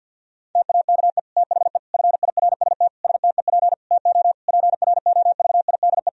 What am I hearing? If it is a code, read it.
Morse code, heard as a keyed tone, about 52 wpm, 700 Hz. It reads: TAKE THE FIRST STEP TO PROVIDE
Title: Sentences from Top 1000 Words - Review Part 4 - 50wpm